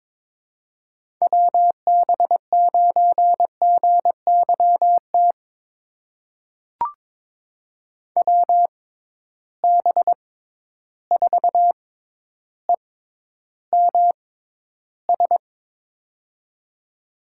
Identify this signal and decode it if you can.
Morse code, heard as a keyed tone, about 22 wpm, spaced as for 11 wpm, 700 Hz. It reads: WB9GYT WB4EMS
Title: USA Call Signs - 6x Character Spacing - 22wpm